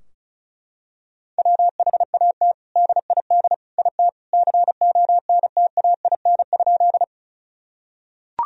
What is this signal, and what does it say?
Morse code, heard as a keyed tone, about 36 wpm, 700 Hz. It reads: WHAT DID IT CONTAIN?
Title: Sentences from Top 1000 Words - Review Part 2 - 35wpm